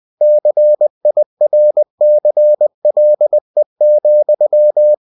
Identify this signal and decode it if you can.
Morse code, heard as a keyed tone, about 20 wpm, 600 Hz. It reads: CIRCLE,
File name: cw-books / winnie-the-pooh / winnie-20wpm-page-0021